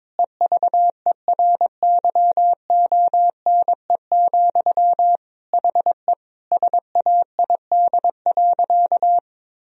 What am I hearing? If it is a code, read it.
Morse code, heard as a keyed tone, about 22 wpm, 700 Hz. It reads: EVERYONE, HE SAID.
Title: Sentences from Top 1000 Words - Review Part 3 - 22wpm